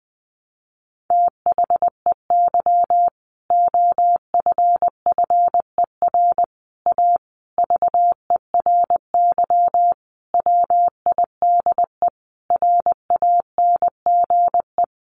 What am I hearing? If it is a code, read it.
Morse code, heard as a keyed tone, about 20 wpm, 700 Hz. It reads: THEY OFFER A VERY WIDE RANGE
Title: Sentences from Top 900 Words 20wpm